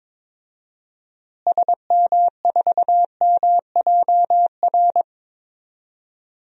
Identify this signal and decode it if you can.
Morse code, heard as a keyed tone, about 22 wpm, 700 Hz. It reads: SM4MJR
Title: DXCC-100 Call Signs - 6 Characters - Repeated 2x - 22wpm